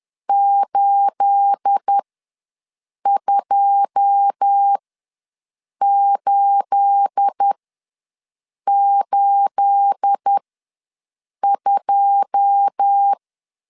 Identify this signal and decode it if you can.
Morse code, heard as a keyed tone, about 10 wpm, spaced as for 6 wpm, 800 Hz. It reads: 82882
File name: LEZIONI / 9 28 ws 10wpm 4124